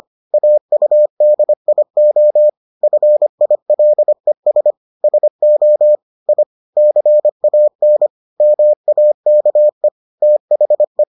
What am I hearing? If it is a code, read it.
Morse code, heard as a keyed tone, about 25 wpm, 600 Hz. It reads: AUDIO FILES SO I CAN MAKE THE